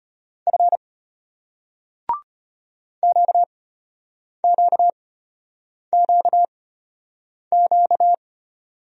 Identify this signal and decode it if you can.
Morse code, heard as a keyed tone, about 33 wpm, spaced as for 8 wpm, 700 Hz. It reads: F QQQQ